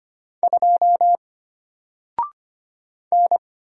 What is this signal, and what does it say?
Morse code, heard as a keyed tone, about 25 wpm, 700 Hz. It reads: 2 N
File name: Single-Letter-Number-25wpm